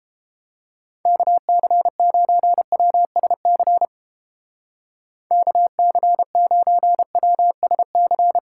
Morse code, heard as a keyed tone, about 30 wpm, 700 Hz. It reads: KC9WSC KC9WSC